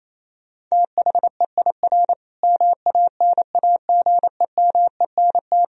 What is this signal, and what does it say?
Morse code, heard as a keyed tone, about 28 wpm, 700 Hz. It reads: THEIR MANAGEMENT
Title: Sentences from Top 1000 Words - Review Part 4 - 28wpm